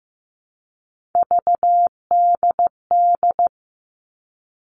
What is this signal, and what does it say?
Morse code, heard as a keyed tone, about 15 wpm, 700 Hz. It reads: VDD
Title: All Cut Numbers - 3-Digit 15wpm